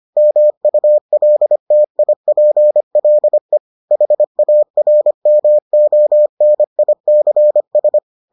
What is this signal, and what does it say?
Morse code, heard as a keyed tone, about 25 wpm, 600 Hz. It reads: MULTIPLE HARMONICS